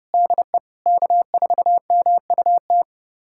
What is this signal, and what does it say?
Morse code, heard as a keyed tone, about 32 wpm, 700 Hz. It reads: DE K4MUT
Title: USA Callsign DE Callsign - 5 Characters - 30wpm